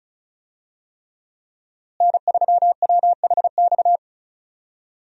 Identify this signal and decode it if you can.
Morse code, heard as a keyed tone, about 36 wpm, 700 Hz. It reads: N3WHX
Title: USA Call Signs - 5 Characters - Repeated 3x - 35wpm